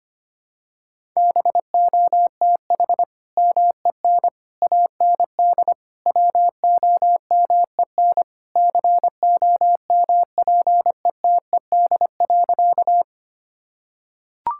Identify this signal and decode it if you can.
Morse code, heard as a keyed tone, about 25 wpm, 700 Hz. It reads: BOTH MEN AND WOMEN COMPETED.
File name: practice / 25wpm / Sentences with 3-Word Phrases - 25wpm